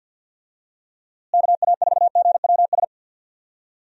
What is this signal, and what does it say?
Morse code, heard as a keyed tone, about 47 wpm, 700 Hz. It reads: KA4GWS
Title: USA Call Signs - 6 Characters - Repeated 2x - 50wpm